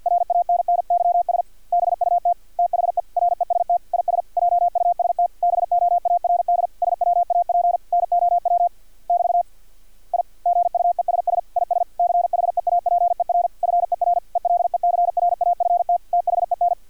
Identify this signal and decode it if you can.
Morse code, heard as a keyed tone, about 50 wpm, 700 Hz. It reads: RANNYS BUT THE REST IS JUST BOARD SWAP NOW <BT> I GUESS IS CHEAPER FER ELECRAFT THEN
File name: archived / CRSnet